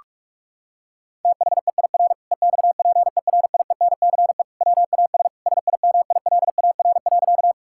Morse code, heard as a keyed tone, about 46 wpm, 700 Hz. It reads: THEIR EXPERIENCE WAS SIMILAR.
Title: Sentences from Top 1000 Words - Review Part 5 - 45wpm